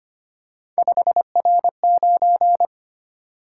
Morse code, heard as a keyed tone, about 31 wpm, 700 Hz. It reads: 5R9
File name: Speed-Racing - Three Letter-Number Combinations - 25wpm